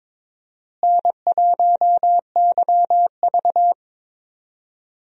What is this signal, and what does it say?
Morse code, heard as a keyed tone, about 22 wpm, 700 Hz. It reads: N1YV
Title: USA Call Signs - 4 Characters - Repeated 2x - 22wpm